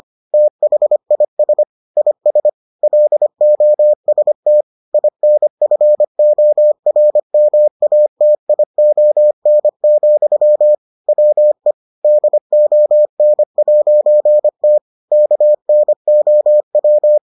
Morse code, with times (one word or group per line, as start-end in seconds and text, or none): 0.3-1.6 THIS
2.0-2.5 IS
2.8-4.6 LOST
4.9-10.8 INFORMATION,
11.1-11.7 WE
12.0-14.8 DON'T
15.1-17.2 KNOW